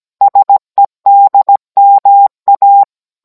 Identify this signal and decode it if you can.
Morse code, heard as a keyed tone, about 17 wpm, 800 Hz. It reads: SEDMA